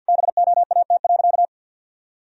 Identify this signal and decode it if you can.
Morse code, heard as a keyed tone, about 50 wpm, 700 Hz. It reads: BOAT.
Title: Sentences from Top 1000 Words - Review Part 4 - 50wpm